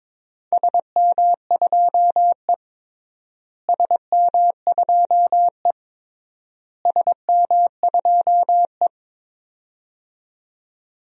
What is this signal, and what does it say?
Morse code, heard as a keyed tone, about 22 wpm, 700 Hz. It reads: SM2E SM2E SM2E
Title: DXCC-100 Call Signs - 4 Characters - Repeated 3x - 22wpm